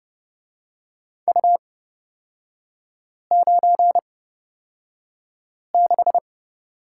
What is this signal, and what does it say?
Morse code, heard as a keyed tone, about 30 wpm, spaced as for 5 wpm, 700 Hz. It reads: U96
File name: Lesson 37e - Letter x - Three Characters 30wpm